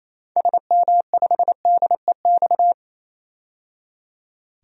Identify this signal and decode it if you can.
Morse code, heard as a keyed tone, about 28 wpm, 700 Hz. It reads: SM5DEX
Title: DXCC-100 Call Signs - 6 Characters - 28wpm